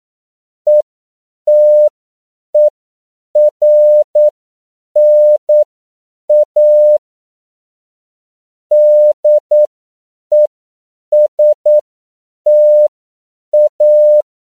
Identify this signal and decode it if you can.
Morse code, heard as a keyed tone, about 9 wpm, spaced as for 7 wpm, 600 Hz. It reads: ETERNA DESTA
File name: CW / 10PPM